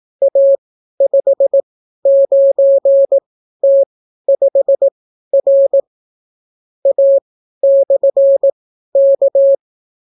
Morse code, heard as a keyed tone, about 18 wpm, spaced as for 12 wpm, 550 Hz. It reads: A59T5R A/K